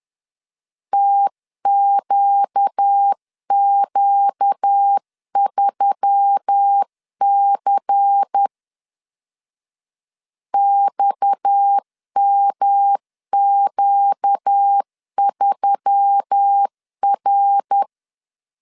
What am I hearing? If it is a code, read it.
Morse code, heard as a keyed tone, about 10 wpm, 800 Hz. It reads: TQQ3C XMQ3R
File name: CW / DET / 1 6 1500C S C 10wpm 4252